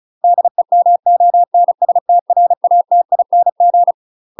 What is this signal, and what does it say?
Morse code, heard as a keyed tone, about 34 wpm, 700 Hz. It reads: DEMONSTRATING
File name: UK MASTER QSOs TRAINING / 35WPM / UK MASTER QSOs 0039